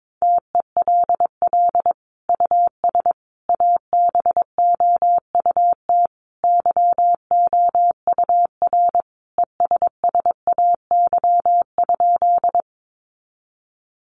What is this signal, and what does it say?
Morse code, heard as a keyed tone, about 22 wpm, 700 Hz. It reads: TELL US ABOUT YOUR ESSAY?
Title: Sentences from Top 1500 Words 22wpm